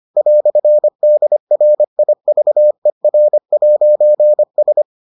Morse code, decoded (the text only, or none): "DRIVER'S